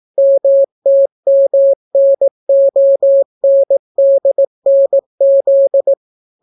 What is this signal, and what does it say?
Morse code, heard as a keyed tone, about 18 wpm, 550 Hz. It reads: MTMNONDNZ